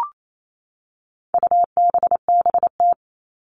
Morse code, heard as a keyed tone, about 28 wpm, 700 Hz. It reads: UBBT
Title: All Cut Numbers - 4-Digit 28wpm